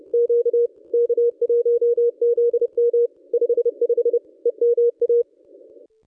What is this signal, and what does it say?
Morse code, heard as a keyed tone, about 29 wpm, 440 Hz. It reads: Q K1ZM 55 EMA
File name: contests / ss-cw-k5tr-op-n5rz -1000